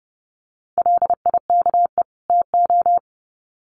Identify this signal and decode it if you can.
Morse code, heard as a keyed tone, about 30 wpm, 700 Hz. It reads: LIKE TO
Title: Sets of 2 Words 30wpm